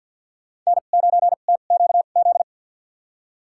Morse code, heard as a keyed tone, about 52 wpm, spaced as for 36 wpm, 700 Hz. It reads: N9TQZ
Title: USA Call Signs - 5 Characters - 2x Character Spacing - 50wpm